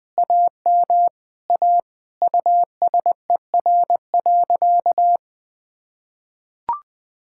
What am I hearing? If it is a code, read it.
Morse code, heard as a keyed tone, about 20 wpm, 700 Hz. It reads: AM A USER.
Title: Sentences from Top 1000 Words - Review Part 5 - 20wpm